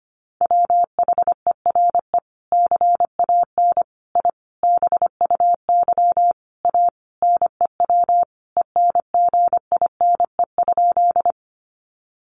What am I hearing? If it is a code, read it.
Morse code, heard as a keyed tone, about 25 wpm, 700 Hz. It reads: WHERE CAN I BUY A NEW ENGINE?
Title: Sentences from Top 1500 Words 25wpm